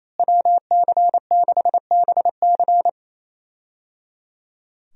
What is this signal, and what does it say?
Morse code, heard as a keyed tone, about 28 wpm, 700 Hz. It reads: WC6BC
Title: USA Call Signs - 5 Characters - Repeated 3x - 28wpm